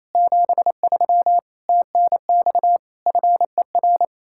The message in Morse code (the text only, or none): 73 TNX FER